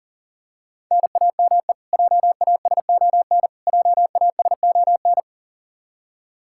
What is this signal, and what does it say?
Morse code, heard as a keyed tone, about 40 wpm, 700 Hz. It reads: NAME JASON JASON
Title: K1USN SST QSO Practice 40wpm